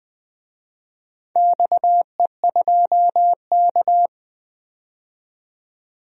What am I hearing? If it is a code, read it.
Morse code, heard as a keyed tone, about 20 wpm, 700 Hz. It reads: XE2K
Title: DXCC-100 Call Signs - 4 Characters - 20wpm